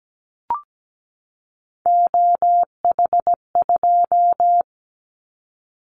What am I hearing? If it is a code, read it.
Morse code, heard as a keyed tone, about 17 wpm, 700 Hz. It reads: OH2
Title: Callsign Prefixes for Top 25 DXCC - 17wpm